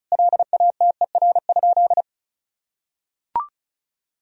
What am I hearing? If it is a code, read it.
Morse code, heard as a keyed tone, about 34 wpm, 700 Hz. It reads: LATER?
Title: Sentences from Top 1000 Words - Review Part 4 - 35wpm